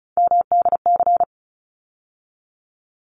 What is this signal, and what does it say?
Morse code, heard as a keyed tone, about 34 wpm, 700 Hz. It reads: MDC
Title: ARRL Sections 35wpm